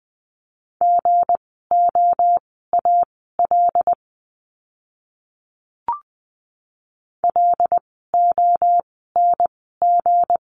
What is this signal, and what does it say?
Morse code, heard as a keyed tone, about 20 wpm, spaced as for 14 wpm, 700 Hz. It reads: GOAL LONG
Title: 160 Lesson 26f - Letter g - Words 20wpm